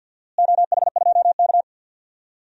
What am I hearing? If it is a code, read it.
Morse code, heard as a keyed tone, about 47 wpm, 700 Hz. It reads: OH2K
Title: DXCC-100 Call Signs - 4 Characters - Repeated 2x - 50wpm